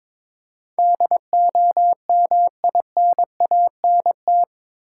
Morse code, heard as a keyed tone, about 22 wpm, 700 Hz. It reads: DOMINANT
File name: practice / 22wpm / Eight Letter Words - 22wpm